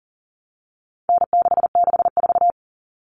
Morse code, heard as a keyed tone, about 40 wpm, 700 Hz. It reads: N664
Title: All Cut Numbers - 4-Digit 40wpm